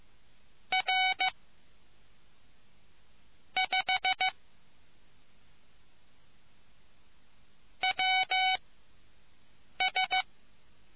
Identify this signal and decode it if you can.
Morse code, heard as a keyed tone, about 15 wpm, 750 Hz. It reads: R 5 W S